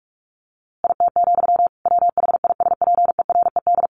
Morse code, heard as a keyed tone, about 44 wpm, 700 Hz. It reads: IT, WHISPERED
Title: A Sound Of Thunder 45wpm